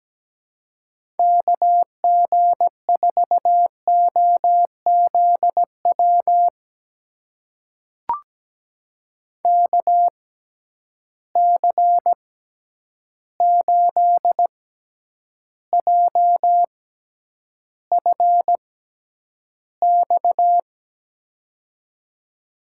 Morse code, heard as a keyed tone, about 17 wpm, spaced as for 9 wpm, 700 Hz. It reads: KG4OZW KC8JFX
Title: USA Call Signs - 6 Characters - 6x Character Spacing - 17wpm